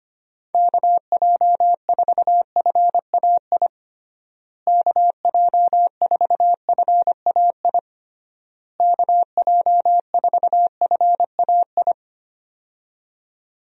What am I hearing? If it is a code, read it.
Morse code, heard as a keyed tone, about 25 wpm, 700 Hz. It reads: KJ4FAI KJ4FAI KJ4FAI
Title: USA Call Signs - 6 Characters - Repeated 3x - 25wpm